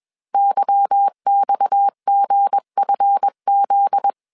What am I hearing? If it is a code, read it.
Morse code, heard as a keyed tone, about 21 wpm, 800 Hz. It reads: YXGFZ